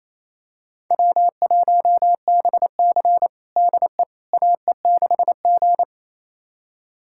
Morse code, heard as a keyed tone, about 28 wpm, 700 Hz. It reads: W1BC DE AE6G
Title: USA Callsign DE Callsign - 4 Characters - 28wpm